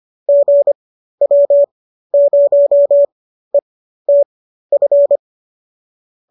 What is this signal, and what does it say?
Morse code, heard as a keyed tone, about 25 wpm, spaced as for 13 wpm, 570 Hz. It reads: GW0ETF